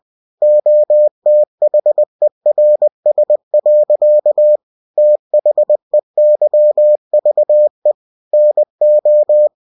Morse code, heard as a keyed tone, about 20 wpm, 600 Hz. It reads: OTHERS. THEYVE NO
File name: cw-books / winnie-the-pooh / winnie-20wpm-page-0088